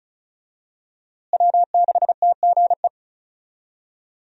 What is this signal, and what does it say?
Morse code, heard as a keyed tone, about 36 wpm, 700 Hz. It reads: W6TGE
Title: USA Call Signs - 5 Characters - Repeated 3x - 35wpm